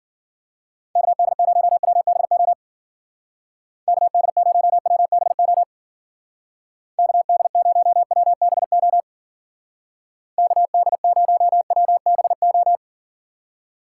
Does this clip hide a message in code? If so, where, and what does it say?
Yes: Morse code from 1.0 to 12.8 s, heard as a keyed tone, about 50 wpm, 700 Hz. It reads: KD0WBO KD0WBO KD0WBO KD0WBO